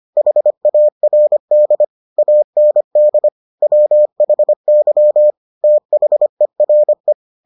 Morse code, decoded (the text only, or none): HARD AND WHY THERE